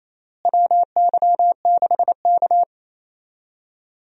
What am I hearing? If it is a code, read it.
Morse code, heard as a keyed tone, about 28 wpm, 700 Hz. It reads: WY6K
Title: USA Call Signs - 4 Characters - Repeated 3x - 28wpm